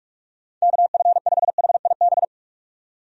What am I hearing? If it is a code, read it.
Morse code, heard as a keyed tone, about 46 wpm, 700 Hz. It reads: KF5HIB
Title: USA Call Signs - 6 Characters - 45wpm